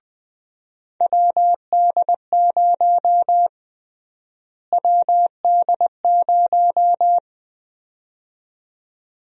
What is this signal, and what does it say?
Morse code, heard as a keyed tone, about 20 wpm, 700 Hz. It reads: WD0 WD0
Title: USA Call Sign Prefixes - Repeated 2x - 20wpm